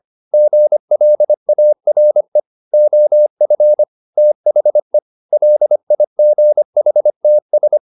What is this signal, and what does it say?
Morse code, heard as a keyed tone, about 25 wpm, 600 Hz. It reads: GLARE OF THE LIGHTS